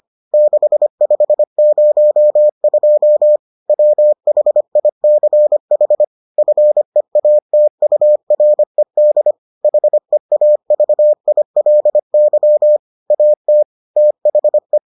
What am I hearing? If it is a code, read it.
Morse code, heard as a keyed tone, about 25 wpm, 600 Hz. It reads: 6502 WHICH FEATURED HEAVILY AT THE